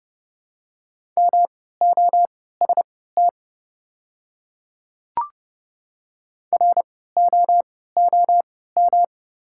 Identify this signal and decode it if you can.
Morse code, heard as a keyed tone, about 30 wpm, spaced as for 17 wpm, 700 Hz. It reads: MOST ROOM
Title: 3x Character Spacing - Top 300 Words - 30wpm